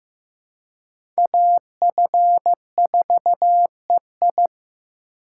Morse code, heard as a keyed tone, about 15 wpm, 700 Hz. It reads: AF4EI
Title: USA Call Signs - 5 Characters - 2x Character Spacing - 15wpm